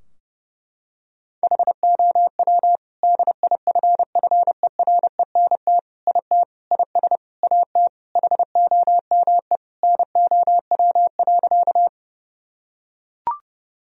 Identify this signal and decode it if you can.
Morse code, heard as a keyed tone, about 32 wpm, 700 Hz. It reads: HOW DIFFERENT IT IS AT HOME NOW.
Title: Sentences from Top 1000 Words - Review Part 3 - 30wpm